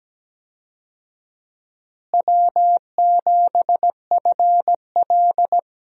Morse code, heard as a keyed tone, about 17 wpm, 700 Hz. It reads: W7FL
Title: USA Call Signs - 4 Characters - Repeated 2x - 17wpm